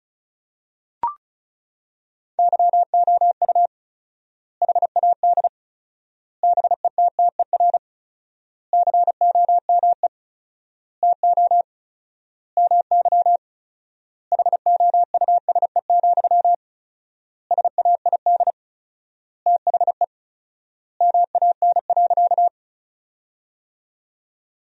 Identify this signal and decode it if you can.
Morse code, heard as a keyed tone, about 34 wpm, 700 Hz. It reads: YOU HAD BETTER COME TO MY HOUSE, SAID THE MAN.
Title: 4x Word-Spacing - Sentences from Top 400 Words - 35wpm